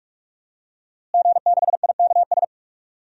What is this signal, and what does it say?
Morse code, heard as a keyed tone, about 46 wpm, 700 Hz. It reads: G6IKS
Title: DXCC-100 Call Signs - 5 Characters - Repeated 2x - 45wpm